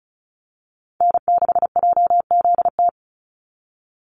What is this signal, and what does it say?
Morse code, heard as a keyed tone, about 34 wpm, 700 Hz. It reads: N6JZT
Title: NA SOTA Activator Callsigns 35wpm